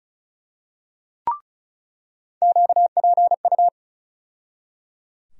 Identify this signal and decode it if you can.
Morse code, heard as a keyed tone, about 36 wpm, 700 Hz. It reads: QPU
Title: USA Call Sign Postfixes - 3 Letters 35wpm